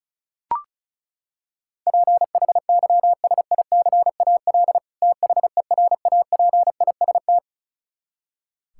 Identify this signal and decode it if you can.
Morse code, heard as a keyed tone, about 34 wpm, 700 Hz. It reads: PHYSICAL THERAPIST
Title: K1USN SST QSO Practice 35wpm